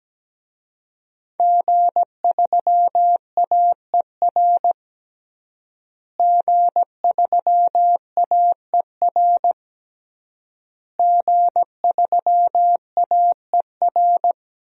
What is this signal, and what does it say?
Morse code, heard as a keyed tone, about 17 wpm, 700 Hz. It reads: G3AER G3AER G3AER
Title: DXCC-100 Call Signs - 5 Characters - Repeated 3x - 17wpm